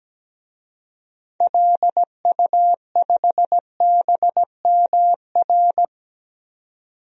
Morse code, heard as a keyed tone, about 17 wpm, 700 Hz. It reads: LU5BMR
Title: DXCC-100 Call Signs - 6 Characters - 17wpm